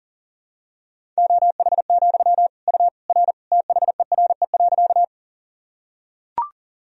Morse code, heard as a keyed tone, about 40 wpm, 700 Hz. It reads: OH, U R THERE.
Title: Sentences from Top 500 Words and Abbreviations 40wpm